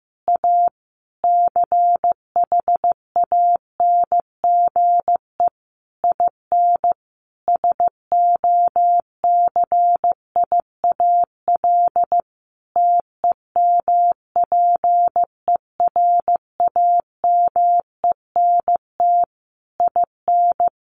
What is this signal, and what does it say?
Morse code, heard as a keyed tone, about 15 wpm, 700 Hz. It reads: A CHANGE IN SOCIAL TEMPERAMENT IN
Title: A Sound Of Thunder 15wpm